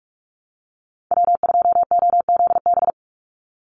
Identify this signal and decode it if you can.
Morse code, heard as a keyed tone, about 44 wpm, 700 Hz. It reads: W2OZB
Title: Call Signs Repeated 2x - 45wpm